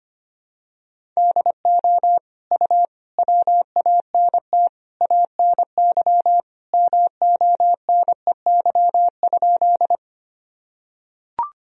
Morse code, heard as a keyed tone, about 25 wpm, 700 Hz. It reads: DO U WANT ANY MONEY?
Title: Sentences from Top 400 Words and Abbreviations 25wpm